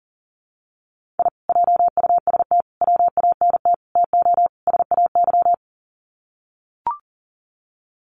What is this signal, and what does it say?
Morse code, heard as a keyed tone, about 40 wpm, 700 Hz. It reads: I JUST WANT TO SAY
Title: Sets of 5 Words Encore 40wpm